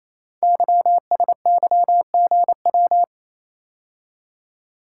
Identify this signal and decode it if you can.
Morse code, heard as a keyed tone, about 29 wpm, 700 Hz. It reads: YSYGW
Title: Code Groups - 5 Letters - 28wpm